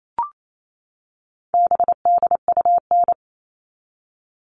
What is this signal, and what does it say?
Morse code, heard as a keyed tone, about 28 wpm, 700 Hz. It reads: BDUN
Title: All Cut Numbers - 4-Digit 28wpm